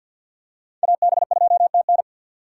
Morse code, heard as a keyed tone, about 48 wpm, 700 Hz. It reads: AB2TN
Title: USA Call Signs - 5 Characters - 3x Character Spacing - 50wpm